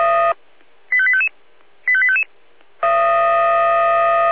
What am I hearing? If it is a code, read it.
Five-tone selective call (ZVEI), heard as quick runs of five tones, tones of about 70 ms. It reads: 85855 85855